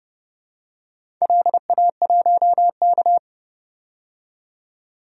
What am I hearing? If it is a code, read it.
Morse code, heard as a keyed tone, about 30 wpm, 700 Hz. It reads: LA1K
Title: DXCC-100 Call Signs - 4 Characters - 30wpm